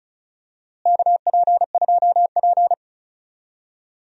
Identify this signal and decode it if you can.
Morse code, heard as a keyed tone, about 36 wpm, 700 Hz. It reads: KP2P
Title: DXCC-100 Call Signs - 4 Characters - Repeated 2x - 35wpm